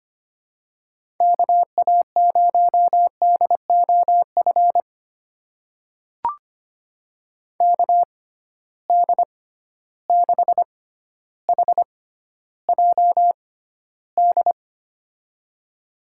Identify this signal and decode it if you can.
Morse code, heard as a keyed tone, about 25 wpm, spaced as for 13 wpm, 700 Hz. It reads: KA0DOF KD6HJD
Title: USA Call Signs - 6 Characters - 6x Character Spacing - 25wpm